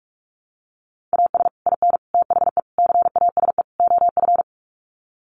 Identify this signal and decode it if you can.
Morse code, heard as a keyed tone, about 44 wpm, 700 Hz. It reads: AS IN THE CASE OF
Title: Sets of 5 Words 45wpm